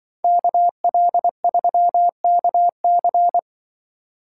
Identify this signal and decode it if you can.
Morse code, heard as a keyed tone, about 23 wpm, 700 Hz. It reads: KL3KC
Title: Speed Racing - USA 2x2 Call Signs 20wpm